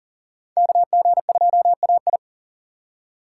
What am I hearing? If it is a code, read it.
Morse code, heard as a keyed tone, about 40 wpm, 700 Hz. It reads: KG2AI
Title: USA Call Signs - 5 Characters - 2x Character Spacing - 40wpm